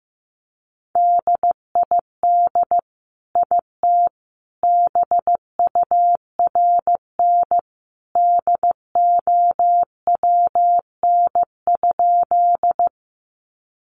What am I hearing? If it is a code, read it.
Morse code, heard as a keyed tone, about 15 wpm, 700 Hz. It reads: DID IT BURN DOWN?
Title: Sentences from Top 1300 Words 15wpm